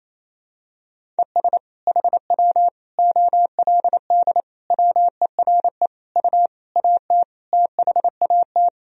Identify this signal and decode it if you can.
Morse code, heard as a keyed tone, about 28 wpm, 700 Hz. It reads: ES HW OLD WERE U AT THAT
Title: Sentences from Top 200 Words and Abbreviations 28wpm